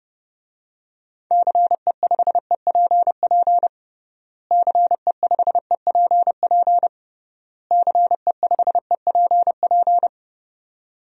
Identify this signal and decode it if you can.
Morse code, heard as a keyed tone, about 30 wpm, 700 Hz. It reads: CE5EPP CE5EPP CE5EPP
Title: DXCC-100 Call Signs - 6 Characters - Repeated 3x - 30wpm